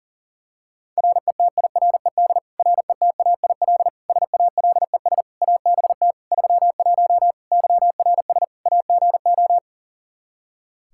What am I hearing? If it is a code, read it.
Morse code, heard as a keyed tone, about 40 wpm, 700 Hz. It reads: RETIRED RETAIL SALES ABT 31 YRS AGO
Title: Practice QSO - Exchange 3 - More Info - 40wpm